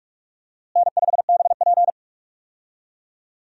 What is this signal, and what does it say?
Morse code, heard as a keyed tone, about 45 wpm, 700 Hz. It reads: N5BP
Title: USA Call Signs - 4 Characters - Repeated 3x - 45wpm